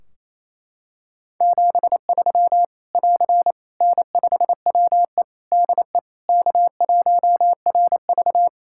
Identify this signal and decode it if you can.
Morse code, heard as a keyed tone, about 27 wpm, 700 Hz. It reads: 73 <AR> N5WE DE K1RV
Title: Practice QSO - Exchange 4 - Ending - 28wpm